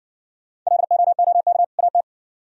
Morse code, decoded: LOOK AT

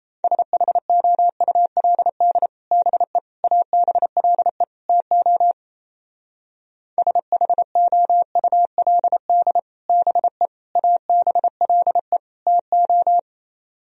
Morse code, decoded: SHOULD BE ABLE TO SHOULD BE ABLE TO

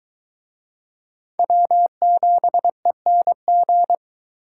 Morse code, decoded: W7ENG

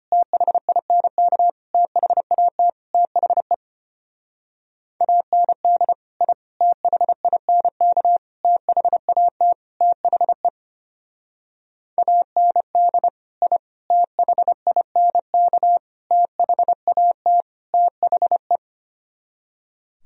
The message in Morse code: THINK THAT THE AND I THINK THAT THE AND I THINK THAT THE

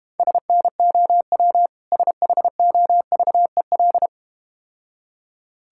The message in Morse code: SNOW SHOVEL

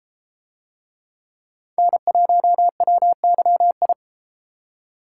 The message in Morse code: N1WYI

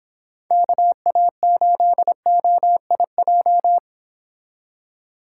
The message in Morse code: KA8OIJ

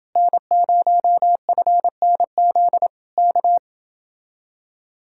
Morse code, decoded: N0FNZ K

